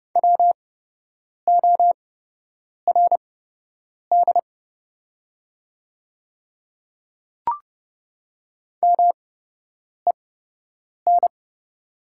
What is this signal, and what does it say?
Morse code, heard as a keyed tone, about 31 wpm, spaced as for 8 wpm, 700 Hz. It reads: WORD MEN
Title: 8x Character Spacing - Top 300 Words - 30wpm